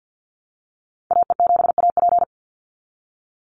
Morse code, heard as a keyed tone, about 50 wpm, 700 Hz. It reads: AE7AP
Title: NA SOTA Activator Callsigns 50wpm